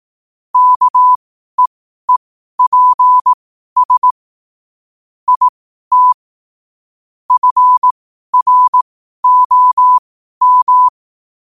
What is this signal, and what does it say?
Morse code, heard as a keyed tone, about 18 wpm, spaced as for 12 wpm, 1000 Hz. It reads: KEEPS IT FROM
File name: qotdfiles / 12wpm